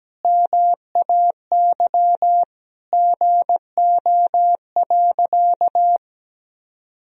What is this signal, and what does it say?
Morse code, heard as a keyed tone, about 17 wpm, 700 Hz. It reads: MAY GO.
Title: Sentences from Top 100 Words and Abbreviations 17wpm